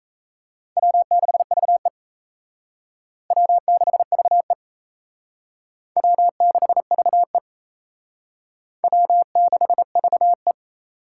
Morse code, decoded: W6VE W6VE W6VE W6VE

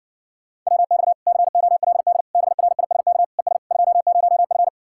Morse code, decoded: WX COLD BRISK ES 20F

